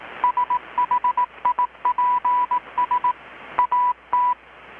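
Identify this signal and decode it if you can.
Morse code, heard as a keyed tone, about 18 wpm, 990 Hz. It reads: SHIPS AT
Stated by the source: Morse from KPH's Night of Nights XX maritime radiotelegraphy event, received in Chiapas, Mexico about 3,800 km away on a Kenwood R-600 with a grounded 10 m long-wire antenna.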